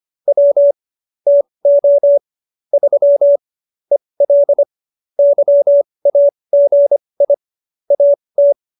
Morse code, heard as a keyed tone, about 25 wpm, spaced as for 20 wpm, 570 Hz. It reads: W TO 3 EL YAGI AT